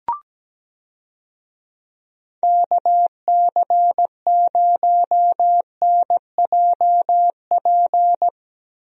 Morse code, heard as a keyed tone, about 17 wpm, 700 Hz. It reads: KC0NJP
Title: USA Call Signs - 6 Characters - Repeated 2x - 17wpm